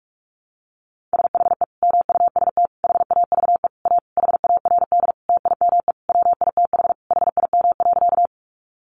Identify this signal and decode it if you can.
Morse code, heard as a keyed tone, about 44 wpm, 700 Hz. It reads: SHE MUST HAVE A HARD TIME WITH HIM.
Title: Sentences from Top 300 Words 45wpm